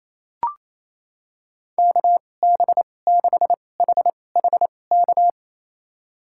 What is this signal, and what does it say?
Morse code, heard as a keyed tone, about 28 wpm, spaced as for 20 wpm, 700 Hz. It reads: KB6HHK
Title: USA Call Signs - 2x Character Spacing - 28wpm